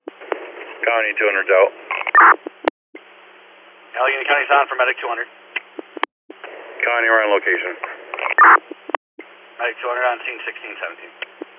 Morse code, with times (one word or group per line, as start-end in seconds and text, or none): none